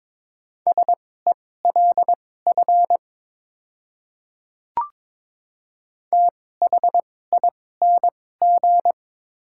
Morse code, heard as a keyed tone, about 22 wpm, spaced as for 16 wpm, 700 Hz. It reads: SELF THING
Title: 2x Character Spacing - Top 200 Words - 22wpm